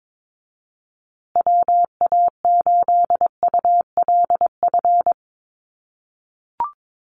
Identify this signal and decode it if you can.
Morse code, heard as a keyed tone, about 22 wpm, 700 Hz. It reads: WA8ULF
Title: USA Call Signs Repeated 2x 22wpm